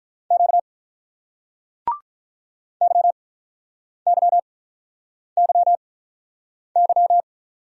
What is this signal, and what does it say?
Morse code, heard as a keyed tone, about 43 wpm, spaced as for 8 wpm, 700 Hz. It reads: Q YYYY
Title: Speed-Racing Single Letters 35wpm